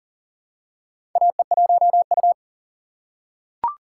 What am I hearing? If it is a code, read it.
Morse code, heard as a keyed tone, about 40 wpm, 700 Hz. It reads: AE1U
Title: USA Call Signs - 4x Character Spacing - 40wpm